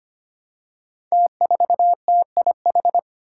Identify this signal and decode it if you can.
Morse code, heard as a keyed tone, about 25 wpm, 700 Hz. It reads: T4TIH
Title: Five Challenging Letter-Number Combinations 25wpm